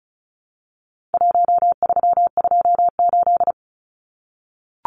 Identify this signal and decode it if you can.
Morse code, heard as a keyed tone, about 34 wpm, 700 Hz. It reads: 1328
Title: Numbers - 4-Digit 35wpm